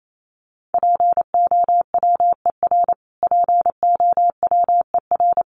Morse code, heard as a keyed tone, about 28 wpm, 700 Hz. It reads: POWER POWER
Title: Top 500 Words Repeated 28wpm